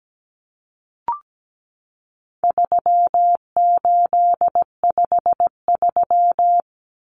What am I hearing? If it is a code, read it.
Morse code, heard as a keyed tone, about 17 wpm, 700 Hz. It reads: 3853
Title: Common Cut Numbers - 4-Digit 17wpm